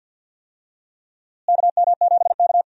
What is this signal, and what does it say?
Morse code, heard as a keyed tone, about 52 wpm, 700 Hz. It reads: KM7K
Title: USA Call Signs - 4 Characters - Repeated 3x - 50wpm